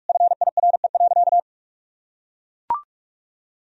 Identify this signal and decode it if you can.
Morse code, heard as a keyed tone, about 44 wpm, 700 Hz. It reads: FIRE.
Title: Sentences from Top 1000 Words - Review Part 3 - 45wpm